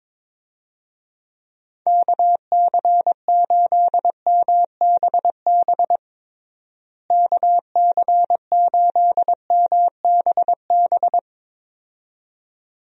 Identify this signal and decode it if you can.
Morse code, heard as a keyed tone, about 21 wpm, 700 Hz. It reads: KC8MBB KC8MBB
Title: USA Call Signs - 6 Characters - Repeated 2x - 22wpm